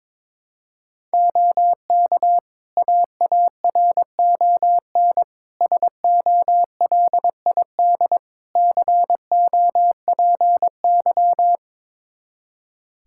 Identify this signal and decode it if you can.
Morse code, heard as a keyed tone, about 22 wpm, 700 Hz. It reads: OK AARON SOLID COPY
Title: K1USN SST QSO Practice 22wpm